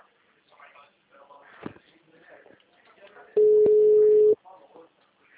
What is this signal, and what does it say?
Ringback tone (Europe, 425 Hz), with speech.
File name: out- -2018- - - .6638